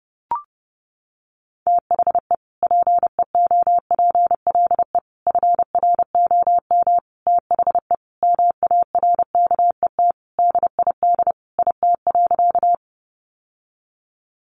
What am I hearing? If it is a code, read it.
Morse code, heard as a keyed tone, about 28 wpm, 700 Hz. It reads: THE PEOPLE FROM THE MARKET DID IT.
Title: Sentences from Top 600 Words 30wpm